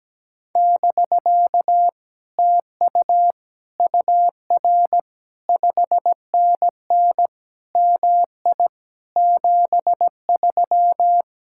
Morse code, heard as a keyed tone, about 17 wpm, 700 Hz. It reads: <BK> TU UR 5NN MI 73
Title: Speed-Racing Common QSO Phrases - 17wpm